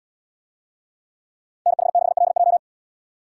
Morse code, heard as a keyed tone, about 70 wpm, 700 Hz. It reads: NH6LJ